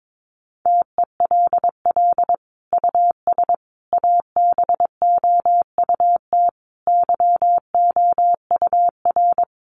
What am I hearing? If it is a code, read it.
Morse code, heard as a keyed tone, about 22 wpm, 700 Hz. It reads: TELL US ABOUT YOUR
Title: Sentences from Top 1200 Words 22wpm